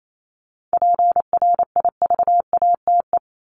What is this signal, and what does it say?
Morse code, heard as a keyed tone, about 28 wpm, 700 Hz. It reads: PRIVATE
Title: Top 751-1000 Words 28wpm